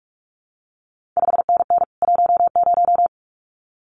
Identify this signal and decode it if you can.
Morse code, heard as a keyed tone, about 46 wpm, 700 Hz. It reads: 5NN 10
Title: CQ WW DX Contest Exchange Encore - 45wpm